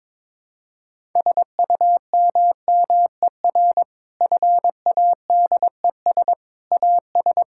Morse code, heard as a keyed tone, about 22 wpm, 700 Hz. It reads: SUMMER FADES AS
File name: Difficult Sentences with Hints - 22wpm